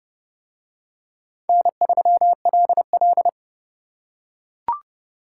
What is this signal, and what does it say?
Morse code, heard as a keyed tone, about 28 wpm, 700 Hz. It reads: N3LL